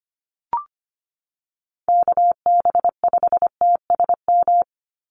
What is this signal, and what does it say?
Morse code, heard as a keyed tone, about 25 wpm, 700 Hz. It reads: KB5TSM
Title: USA Call Signs 25wpm